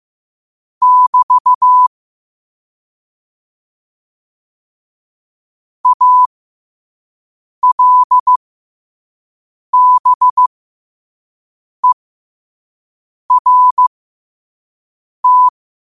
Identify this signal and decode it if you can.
Morse code, heard as a keyed tone, about 15 wpm, spaced as for 5 wpm, 1000 Hz. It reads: <BT> ALBERT
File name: 5wpm